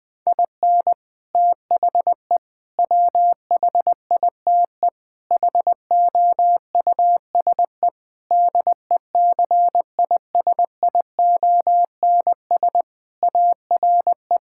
Morse code, heard as a keyed tone, about 21 wpm, 700 Hz. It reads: IN THE WHITE HOUSE DECISIONS ARE
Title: Sentences with 4-Word Phrases - 20wpm